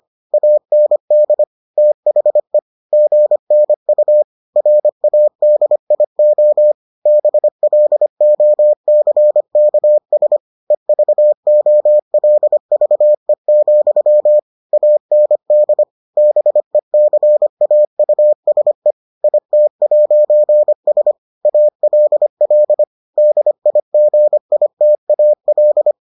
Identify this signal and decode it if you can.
Morse code, heard as a keyed tone, about 25 wpm, 600 Hz. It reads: AND THE GNU RADIO BLOCKS EVOLVE, AND BECAUSE IT'S ALL DIGITAL